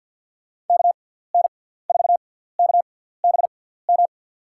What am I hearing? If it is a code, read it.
Morse code, heard as a keyed tone, about 53 wpm, spaced as for 17 wpm, 700 Hz. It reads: KN4KBM
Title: USA Call Signs - 6 Characters - 6x Character Spacing - 50wpm